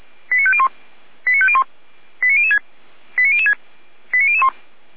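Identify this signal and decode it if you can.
Five-tone selective call (ZVEI), heard as quick runs of five tones, tones of about 80 ms. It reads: 89501 89501 89006 89006 89001
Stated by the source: ZVEI five-tone selective call received by SDR.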